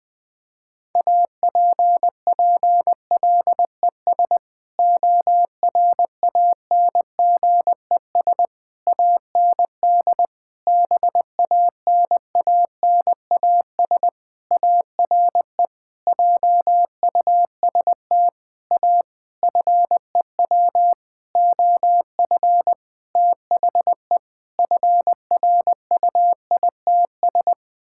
Morse code, heard as a keyed tone, about 20 wpm, 700 Hz. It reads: APPLES ORANGES AND BANANAS ARE JUST A FEW OF THE FRUITS
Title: Sentences with 6-Word Phrases - Spoken Beginning - 20wpm